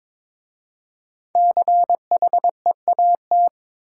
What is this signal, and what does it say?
Morse code, heard as a keyed tone, about 22 wpm, 700 Hz. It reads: CHEAT
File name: 5 Letter Words - First 3 Letters Given - 22wpm